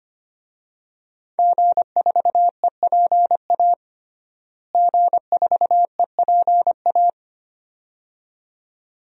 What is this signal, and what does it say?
Morse code, heard as a keyed tone, about 25 wpm, 700 Hz. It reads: G4EPA G4EPA
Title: DXCC-100 Call Signs - 5 Characters - Repeated 2x - 25wpm